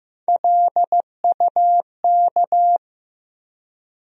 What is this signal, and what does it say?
Morse code, heard as a keyed tone, about 15 wpm, 700 Hz. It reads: LUK
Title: USA Call Sign Postfixes - 3 Letters 15wpm